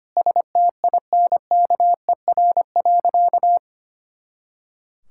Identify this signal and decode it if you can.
Morse code, heard as a keyed tone, about 25 wpm, 700 Hz. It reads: STINKER.
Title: Pangram Challenge v2 - 25wpm